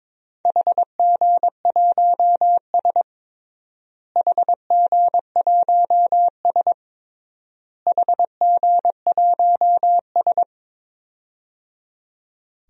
Morse code, heard as a keyed tone, about 22 wpm, 700 Hz. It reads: HG1S HG1S HG1S
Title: DXCC-100 Call Signs - 4 Characters - Repeated 3x - 22wpm